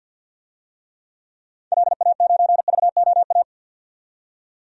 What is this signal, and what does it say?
Morse code, heard as a keyed tone, about 51 wpm, 700 Hz. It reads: LA9VOA
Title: DXCC-100 Call Signs - 6 Characters - 50wpm